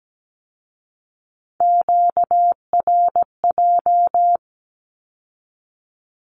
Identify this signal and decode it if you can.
Morse code, heard as a keyed tone, about 17 wpm, 700 Hz. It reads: QRJ
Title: Q-Codes with Spoken Meaning - 17wpm